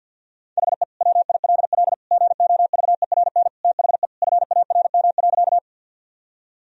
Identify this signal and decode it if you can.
Morse code, heard as a keyed tone, about 52 wpm, 700 Hz. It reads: HE WILL GOVERN THE FARM.
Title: Sentences from Top 1000 Words - Review Part 2 - 50wpm